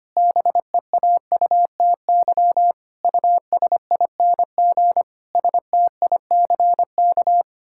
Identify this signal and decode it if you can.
Morse code, heard as a keyed tone, about 25 wpm, 700 Hz. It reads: BEAUTY USING STICK
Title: Pangram Challenge v2 - 25wpm